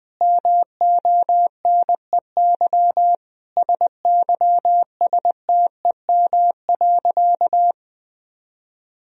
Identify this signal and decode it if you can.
Morse code, heard as a keyed tone, about 20 wpm, 700 Hz. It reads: MONEY SYSTEM.
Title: Sentences from Top 1000 Words - Review Part 4 - 20wpm